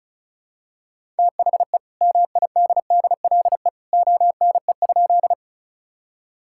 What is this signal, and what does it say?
Morse code, heard as a keyed tone, about 36 wpm, 700 Hz. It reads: THE MIDDLE ONE?
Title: Sentences from Top 1000 Words - Review Part 6 - 35wpm